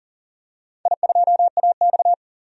Morse code, heard as a keyed tone, about 41 wpm, 700 Hz. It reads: I2AX
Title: DXCC-100 Call Signs - 4 Characters - 40wpm